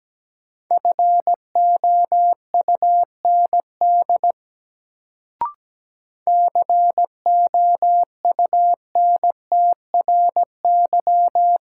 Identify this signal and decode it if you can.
Morse code, heard as a keyed tone, about 17 wpm, 700 Hz. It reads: FOUND COUNTRY
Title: Sentences from Top 300 Words - New Word Drill - 17wpm